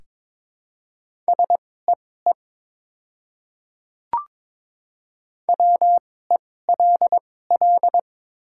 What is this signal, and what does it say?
Morse code, heard as a keyed tone, about 22 wpm, spaced as for 16 wpm, 700 Hz. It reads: SEE WELL
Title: 2x Character Spacing - Top 200 Words - 22wpm